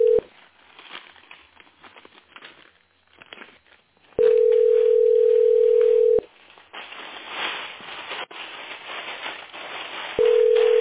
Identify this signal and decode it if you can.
Ringback tone (North America), 440 + 480 Hz.